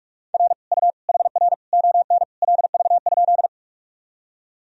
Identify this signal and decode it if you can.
Morse code, heard as a keyed tone, about 44 wpm, 700 Hz. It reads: R U HR ON LV?
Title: Sentences from Top 400 Words and Abbreviations 45wpm